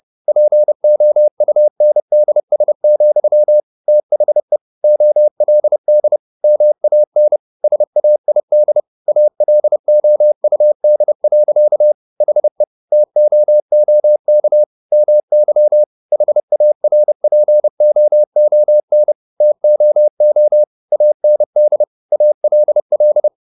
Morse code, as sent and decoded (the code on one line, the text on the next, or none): .--. --- ..- -. -.. ... --..-- / - .... / --- .-.. -.. / -- .- -. / ... .- .. -.. / .- .-.. --- ..- -.. .-.-.- / .... / - --- --- -.- / -- -.-- / .... .- .-. .--. --- --- -. / - --- --- / .- -. -.. / .- .-.. .-..
POUNDS, THE OLD MAN SAID ALOUD. HE TOOK MY HARPOON TOO AND ALL